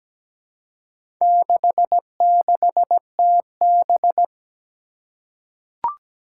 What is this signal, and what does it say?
Morse code, heard as a keyed tone, about 17 wpm, 700 Hz. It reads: 66TB
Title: Four Challenging Letter-Number Combinations 17wpm